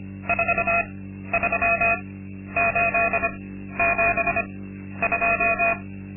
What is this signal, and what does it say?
Morse code, heard as a keyed tone, about 25 wpm, spaced as for 12 wpm, 680 Hz. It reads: 43872
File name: Linksmakalnis-2